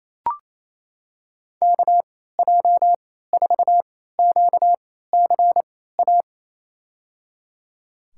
Morse code, heard as a keyed tone, about 28 wpm, spaced as for 16 wpm, 700 Hz. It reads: KJ4QCA
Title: USA Call Signs - 3x Character Spacing - 28wpm